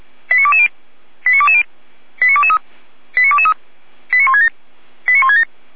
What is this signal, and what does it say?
Five-tone selective call (ZVEI), heard as quick runs of five tones, tones of about 70 ms. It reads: 89220 89220 89223 89223 89167 89167